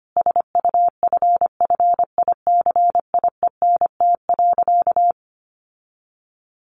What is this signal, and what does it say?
Morse code, heard as a keyed tone, about 25 wpm, 700 Hz. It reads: SUFFICIENT.
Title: Sentences from Top 1400 Words 25wpm